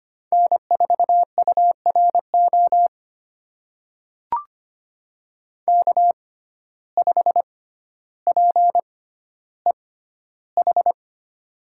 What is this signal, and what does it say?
Morse code, heard as a keyed tone, about 25 wpm, spaced as for 13 wpm, 700 Hz. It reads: N4URO K5PEH
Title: USA Call Signs - 5 Characters - 6x Character Spacing - 25wpm